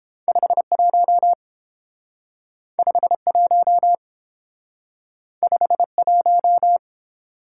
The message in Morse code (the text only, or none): 51 51 51